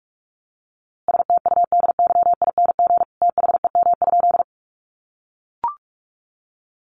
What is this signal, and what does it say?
Morse code, heard as a keyed tone, about 44 wpm, 700 Hz. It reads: STUDYING THEM?
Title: A Sound Of Thunder 45wpm